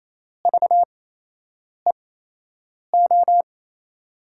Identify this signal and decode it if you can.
Morse code, heard as a keyed tone, about 27 wpm, spaced as for 8 wpm, 700 Hz. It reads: VEO
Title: USA Call Signs - 6 Characters - 8x Character Spacing - 28wpm